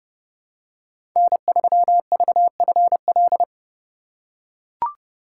Morse code, heard as a keyed tone, about 29 wpm, 700 Hz. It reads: N3VFL
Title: USA Call Signs - 5 Characters - 2x Character Spacing - 30wpm